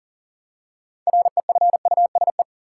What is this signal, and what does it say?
Morse code, heard as a keyed tone, about 40 wpm, 700 Hz. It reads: REFUSE
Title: Six Letter Words - 40wpm